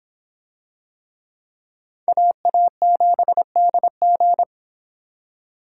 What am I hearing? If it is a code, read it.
Morse code, heard as a keyed tone, about 26 wpm, 700 Hz. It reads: AA7DG